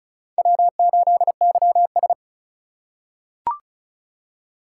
Morse code, heard as a keyed tone, about 36 wpm, 700 Hz. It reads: W8YS
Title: USA Call Signs - 4 Characters - 4x Character Spacing - 35wpm